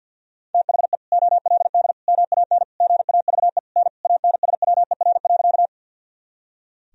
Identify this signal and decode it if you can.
Morse code, heard as a keyed tone, about 50 wpm, 700 Hz. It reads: THE OLD MAN GAVE N ANSWER.
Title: Sentences from Top 500 Words and Abbreviations 50wpm